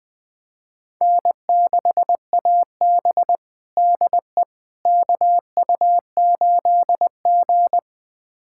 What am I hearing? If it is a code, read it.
Morse code, heard as a keyed tone, about 21 wpm, 700 Hz. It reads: N6AB DE KU8G
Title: USA Callsign DE Callsign - 4 Characters - 20wpm